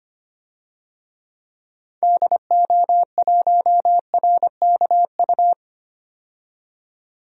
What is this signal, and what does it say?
Morse code, heard as a keyed tone, about 25 wpm, 700 Hz. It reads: DO1RKU
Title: DXCC-100 Call Signs - 6 Characters - 25wpm